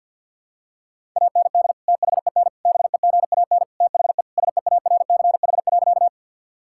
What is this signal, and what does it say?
Morse code, heard as a keyed tone, about 50 wpm, 700 Hz. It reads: AND THEN BEGAN THE SEARCH.
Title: Sentences from Top 1000 Words - Review Part 5 - 50wpm